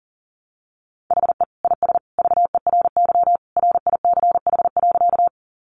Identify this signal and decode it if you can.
Morse code, heard as a keyed tone, about 40 wpm, 700 Hz. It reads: HE IS VERY RICH.
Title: Sentences from Top 1100 Words 40wpm